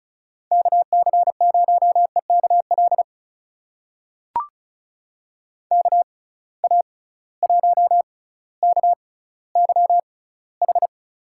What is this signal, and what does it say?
Morse code, heard as a keyed tone, about 36 wpm, spaced as for 18 wpm, 700 Hz. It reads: KC0EKL KA1KYH
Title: USA Call Signs - 6 Characters - 6x Character Spacing - 35wpm